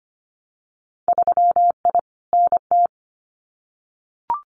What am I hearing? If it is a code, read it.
Morse code, heard as a keyed tone, about 25 wpm, 700 Hz. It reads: 3I NT
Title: Winter Field Day Exchanges Encore 25wpm